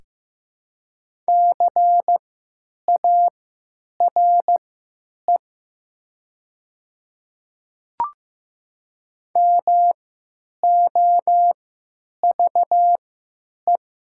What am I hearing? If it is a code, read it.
Morse code, heard as a keyed tone, about 15 wpm, spaced as for 9 wpm, 700 Hz. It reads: CARE MOVE
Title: 3x Character Spacing - Top 300 Words - 15wpm